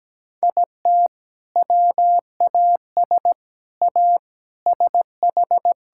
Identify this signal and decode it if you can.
Morse code, heard as a keyed tone, about 17 wpm, 700 Hz. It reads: IT WAS A SH
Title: Sentences from Top 1600 Words 17wpm